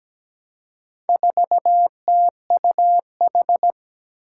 Morse code, heard as a keyed tone, about 17 wpm, 700 Hz. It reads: 4TUH
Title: Four Challenging Letter-Number Combinations 17wpm